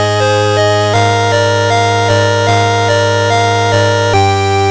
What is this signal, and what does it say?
There is no code.